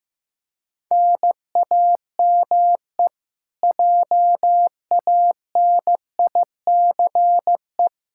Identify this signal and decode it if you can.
Morse code, heard as a keyed tone, about 15 wpm, 700 Hz. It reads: NAME JANICE
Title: Practice QSO - Exchange 1 - Beginning - 15wpm